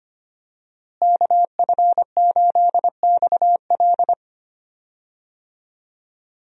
Morse code, heard as a keyed tone, about 25 wpm, 700 Hz. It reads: KF8XL
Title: USA Call Signs - 5 Characters - Repeated 2x - 25wpm